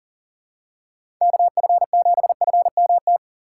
Morse code, heard as a keyed tone, about 40 wpm, 700 Hz. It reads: KF7FMT
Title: USA Call Signs - 6 Characters - 6x Character Spacing - 40wpm